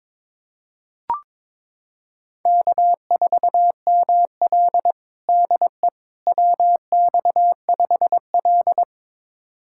Morse code, heard as a keyed tone, about 22 wpm, 700 Hz. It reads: K4ML DE WX5L
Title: USA Callsign DE Callsign - 4 Characters - 22wpm